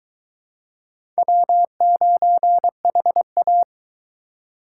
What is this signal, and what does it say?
Morse code, heard as a keyed tone, about 22 wpm, 700 Hz. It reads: W9HA